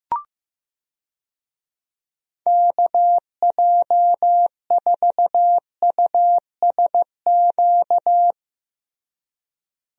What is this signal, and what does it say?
Morse code, heard as a keyed tone, about 16 wpm, 700 Hz. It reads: KJ4USQ
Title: USA Call Signs - 6 Characters - Repeated 2x - 15wpm